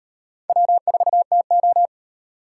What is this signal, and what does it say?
Morse code, heard as a keyed tone, about 38 wpm, 700 Hz. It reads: W4TO